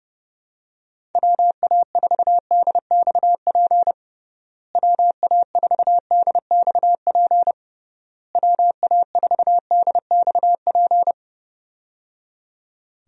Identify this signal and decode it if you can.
Morse code, heard as a keyed tone, about 30 wpm, 700 Hz. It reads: WA4DXP WA4DXP WA4DXP
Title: USA Call Signs - 6 Characters - Repeated 3x - 30wpm